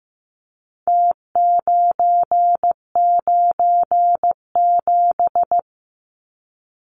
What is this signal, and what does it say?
Morse code, heard as a keyed tone, about 15 wpm, 700 Hz. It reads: T997
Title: Numbers - Contest Style - 4-Digit 15wpm